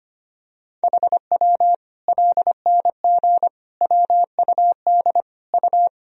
Morse code, heard as a keyed tone, about 25 wpm, 700 Hz. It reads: HW LNG WUD U
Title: Sentences from Top 500 Words and Abbreviations 25wpm